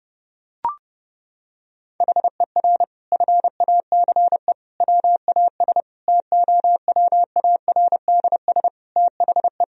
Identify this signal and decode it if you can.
Morse code, heard as a keyed tone, about 30 wpm, 700 Hz. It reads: HER FACE WAS TOWARDS THE
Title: Sentences from Top 1000 Words - Review Part 2 - 30wpm